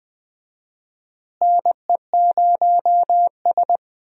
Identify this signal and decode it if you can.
Morse code, heard as a keyed tone, about 20 wpm, 700 Hz. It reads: NE0S
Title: USA Call Signs - 4 Characters - Repeated 3x - 20wpm